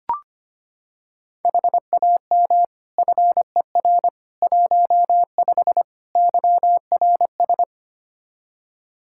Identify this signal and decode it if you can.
Morse code, heard as a keyed tone, about 25 wpm, 700 Hz. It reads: HAM FER 15 YRS
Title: K1USN SST QSO Practice 25wpm